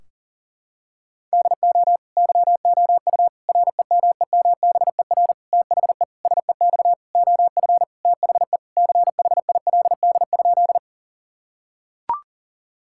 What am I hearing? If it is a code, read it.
Morse code, heard as a keyed tone, about 40 wpm, 700 Hz. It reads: DO YOU REMEMBER THE SEX OF THE CHILD?
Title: Sentences from Top 1600 Words 40wpm